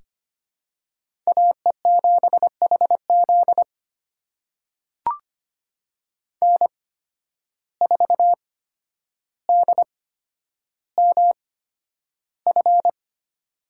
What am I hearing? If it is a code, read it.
Morse code, heard as a keyed tone, about 25 wpm, spaced as for 11 wpm, 700 Hz. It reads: AE7HZ N4DMF
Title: USA Call Signs - 5 Characters - 8x Character Spacing - 25wpm